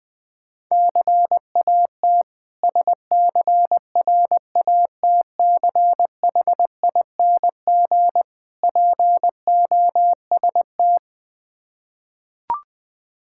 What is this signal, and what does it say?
Morse code, heard as a keyed tone, about 20 wpm, 700 Hz. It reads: CAT SCRATCHING POST
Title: Difficult Words with Hints - 20wpm